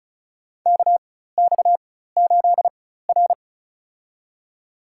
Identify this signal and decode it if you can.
Morse code, heard as a keyed tone, about 36 wpm, spaced as for 16 wpm, 700 Hz. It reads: KX8R
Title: USA Call Signs - 4 Characters - 4x Character Spacing - 35wpm